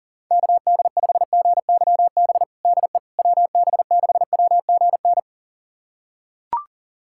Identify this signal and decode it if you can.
Morse code, heard as a keyed tone, about 40 wpm, 700 Hz. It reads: KD5GYB DE WB6WGN
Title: USA Callsign DE Callsign - 6 Characters - 40wpm